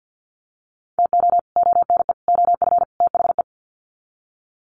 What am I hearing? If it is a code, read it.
Morse code, heard as a keyed tone, about 48 wpm, 700 Hz. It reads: TO ONE OF THE
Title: Sets of 4 Words 50wpm